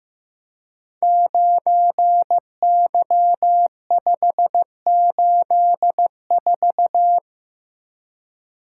Morse code, heard as a keyed tone, about 15 wpm, 700 Hz. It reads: 9Y584
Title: Code Groups - 5 Letter:Numbers - 15wpm